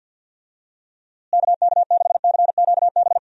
Morse code, heard as a keyed tone, about 50 wpm, 700 Hz. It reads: KK6CQB
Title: USA Call Signs - 6 Characters - Repeated 2x - 50wpm